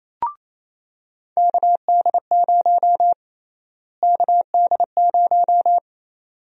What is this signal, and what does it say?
Morse code, heard as a keyed tone, about 28 wpm, 700 Hz. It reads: KD0 KD0
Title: USA Call Sign Prefixes - Repeated 2x - 28wpm